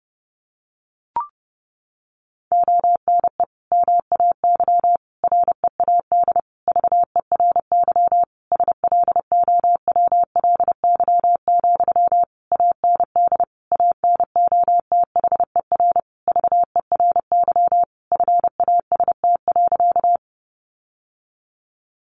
Morse code, heard as a keyed tone, about 30 wpm, 700 Hz. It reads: ONE MAY READ VERY SLOWLY, AND ANOTHER VERY FAST.
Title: Sentences from Top 1100 Words 30wpm